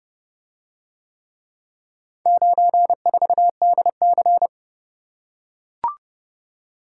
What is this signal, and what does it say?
Morse code, heard as a keyed tone, about 31 wpm, 700 Hz. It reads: 94DC